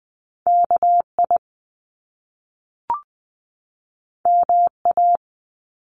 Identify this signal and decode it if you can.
Morse code, heard as a keyed tone, about 20 wpm, 700 Hz. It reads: K I M A
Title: Top Two Letter Combinations in Words 20wpm